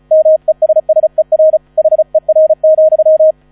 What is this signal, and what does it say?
Morse code, heard as a keyed tone, about 34 wpm, 630 Hz. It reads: MESSER HER,
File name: 35wpm